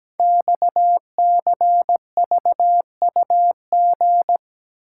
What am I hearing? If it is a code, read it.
Morse code, heard as a keyed tone, about 17 wpm, 700 Hz. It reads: XCVUG
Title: Code Groups - 5 Letter:Numbers - 17wpm